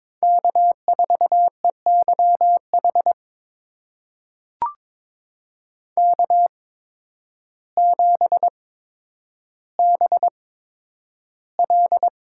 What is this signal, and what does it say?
Morse code, heard as a keyed tone, about 22 wpm, spaced as for 10 wpm, 700 Hz. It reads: K4EYH K7BL
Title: USA Call Signs - 5 Characters - 8x Character Spacing - 22wpm